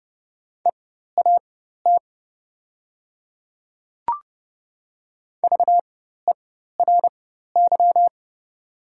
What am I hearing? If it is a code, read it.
Morse code, heard as a keyed tone, about 30 wpm, spaced as for 14 wpm, 700 Hz. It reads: EAT VERY
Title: 4x Character Spacing - Top 300 Words - 30wpm